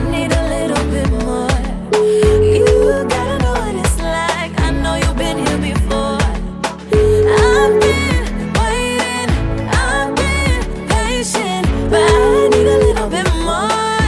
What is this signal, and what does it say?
Ringback tone (Europe, 425 Hz), with noise.